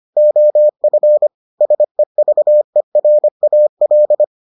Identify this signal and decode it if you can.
Morse code, heard as a keyed tone, about 25 wpm, 600 Hz. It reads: OF SEVERAL